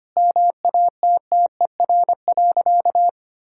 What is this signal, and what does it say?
Morse code, heard as a keyed tone, about 25 wpm, 700 Hz. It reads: MATTER.
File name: practice / 25wpm / Sentences with 4-Word Phrases - 25wpm